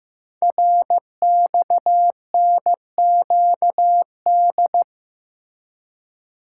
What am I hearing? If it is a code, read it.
Morse code, heard as a keyed tone, about 15 wpm, 700 Hz. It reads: RXNQD
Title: Code Groups - 5 Letters - 15wpm